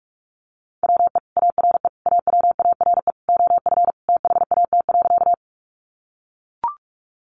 Morse code, heard as a keyed tone, about 44 wpm, 700 Hz. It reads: WE ARE AWARE OF THAT.
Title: Sentences from Top 1200 Words 45wpm